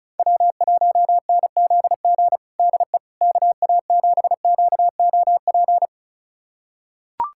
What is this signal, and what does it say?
Morse code, heard as a keyed tone, about 34 wpm, 700 Hz. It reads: W1NZG DE KA7QOP
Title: USA Callsign DE Callsign 35wpm